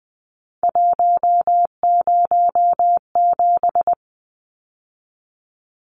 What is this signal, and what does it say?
Morse code, heard as a keyed tone, about 20 wpm, 700 Hz. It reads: 107
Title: Numbers - 3-Digit 20wpm